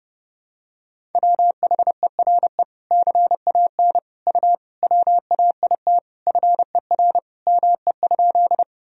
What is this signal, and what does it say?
Morse code, heard as a keyed tone, about 32 wpm, 700 Hz. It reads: WHERE CAN U WAIT FER ME?
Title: Sentences from Top 500 Words and Abbreviations 30wpm